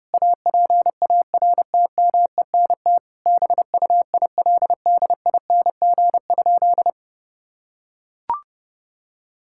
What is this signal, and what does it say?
Morse code, heard as a keyed tone, about 32 wpm, 700 Hz. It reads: APARTMENT BUILDING?
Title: Sentences from Top 1000 Words - Review Part 6 - 30wpm